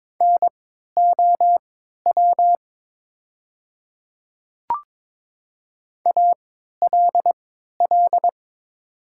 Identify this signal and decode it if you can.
Morse code, heard as a keyed tone, about 23 wpm, spaced as for 13 wpm, 700 Hz. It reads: NOW ALL
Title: 3x Character Spacing - Top 500 Words - 22wpm